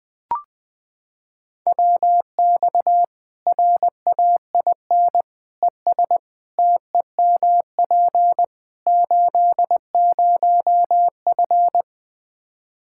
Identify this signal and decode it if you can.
Morse code, heard as a keyed tone, about 20 wpm, 700 Hz. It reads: WX RAIN ES TEMP 80F
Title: Practice QSO - Exchange 2 - Info - 20wpm